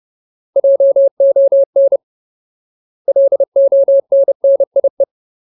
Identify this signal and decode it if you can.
Morse code, heard as a keyed tone, about 30 wpm, 550 Hz. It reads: JON LONNIE
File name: Ultimate Word List / 30WPM Ultimate